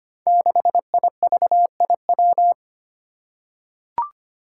Morse code, heard as a keyed tone, about 25 wpm, 700 Hz. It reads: 6IVIW
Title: Five Challenging Letter-Number Combinations 25wpm